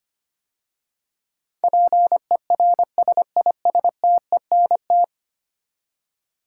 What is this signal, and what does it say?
Morse code, heard as a keyed tone, about 25 wpm, 700 Hz. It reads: PERSISTENT
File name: 10 Letter Words - First 3 Letters Given - 25wpm